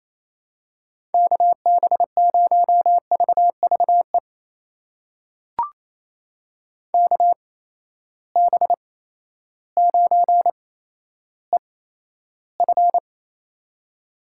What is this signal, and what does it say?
Morse code, heard as a keyed tone, about 28 wpm, spaced as for 13 wpm, 700 Hz. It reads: KB0VVE KB9EF
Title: USA Call Signs - 8x Character Spacing - 28wpm